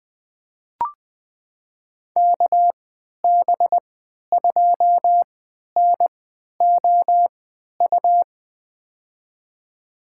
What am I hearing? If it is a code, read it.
Morse code, heard as a keyed tone, about 20 wpm, spaced as for 11 wpm, 700 Hz. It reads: KB2NOU
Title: USA Call Signs - 3x Character Spacing - 20wpm